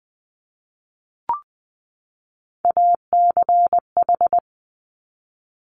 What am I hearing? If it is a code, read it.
Morse code, heard as a keyed tone, about 20 wpm, 700 Hz. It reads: ACH
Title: Top Three Letter Combinations in Words 20wpm